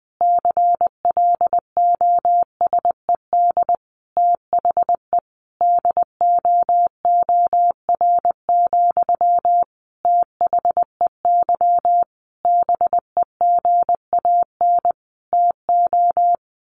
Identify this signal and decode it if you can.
Morse code, heard as a keyed tone, about 20 wpm, 700 Hz. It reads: CLOSED THE DOOR, THEY BEGAN TO
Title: Sentences from Top 1400 Words 20wpm